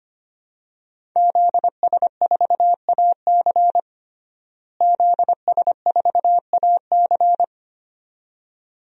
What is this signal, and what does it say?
Morse code, heard as a keyed tone, about 25 wpm, 700 Hz. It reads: ZS4AC ZS4AC
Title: DXCC-100 Call Signs - 5 Characters - Repeated 2x - 25wpm